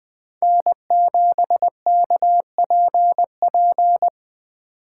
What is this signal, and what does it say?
Morse code, heard as a keyed tone, about 20 wpm, 700 Hz. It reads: N7KPP
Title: USA Call Signs - 5 Characters - 2x Character Spacing - 20wpm